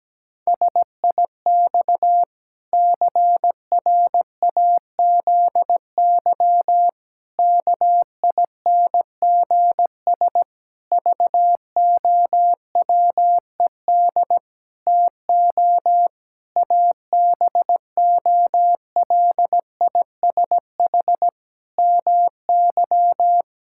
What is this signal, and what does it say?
Morse code, heard as a keyed tone, about 17 wpm, 700 Hz. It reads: SIX CRAZY KINGS VOWED TO ABOLISH MY
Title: Pangram Challenge v2 - 17wpm